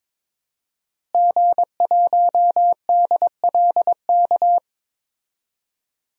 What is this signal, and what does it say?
Morse code, heard as a keyed tone, about 22 wpm, 700 Hz. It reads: G1DLK
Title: DXCC-100 Call Signs - 5 Characters - Repeated 2x - 22wpm